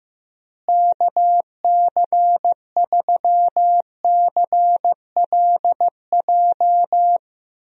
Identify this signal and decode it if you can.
Morse code, heard as a keyed tone, about 15 wpm, 700 Hz. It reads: KC3CLJ
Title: USA Call Signs - 6 Characters - Repeated 3x - 15wpm